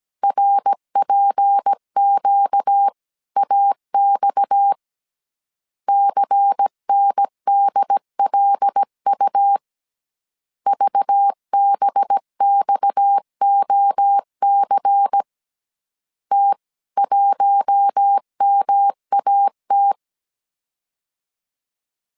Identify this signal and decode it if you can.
Morse code, heard as a keyed tone, about 17 wpm, 800 Hz. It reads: RPQAX CNDLU VBXOC T1MAT
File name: CW / DET / 1 6 1500C S A 16wpm 2635